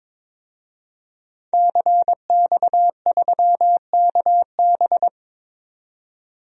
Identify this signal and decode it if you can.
Morse code, heard as a keyed tone, about 22 wpm, 700 Hz. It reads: CX3KB
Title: DXCC-100 Call Signs - 5 Characters - 22wpm